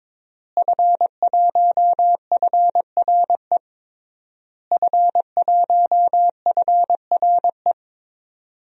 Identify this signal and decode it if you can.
Morse code, heard as a keyed tone, about 22 wpm, 700 Hz. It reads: F1FRE F1FRE
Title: DXCC-100 Call Signs - 5 Characters - Repeated 3x - 22wpm